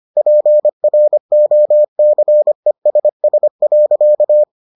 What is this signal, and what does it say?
Morse code, heard as a keyed tone, about 25 wpm, 600 Hz. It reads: PROCESS.